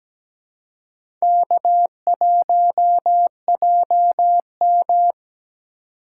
Morse code, heard as a keyed tone, about 17 wpm, 700 Hz. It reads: K1JM
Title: USA Call Signs - 4 Characters - 2x Character Spacing - 17wpm